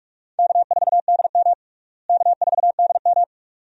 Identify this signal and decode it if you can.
Morse code, heard as a keyed tone, about 46 wpm, 700 Hz. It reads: K4DM K4DM
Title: USA Call Signs - 4 Characters - Repeated 2x - 45wpm